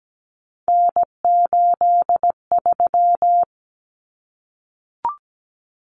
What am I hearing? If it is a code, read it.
Morse code, heard as a keyed tone, about 17 wpm, 700 Hz. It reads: N83
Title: Common Cut Numbers - 3-Digit 17wpm